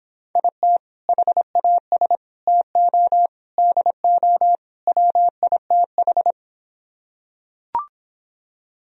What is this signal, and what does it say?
Morse code, heard as a keyed tone, about 26 wpm, 700 Hz. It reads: IT HAS TO DO WITH